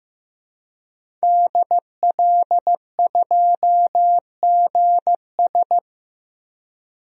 Morse code, heard as a keyed tone, about 15 wpm, 700 Hz. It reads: DL2GS
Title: DXCC-100 Call Signs - 5 Characters - Repeated 3x - 15wpm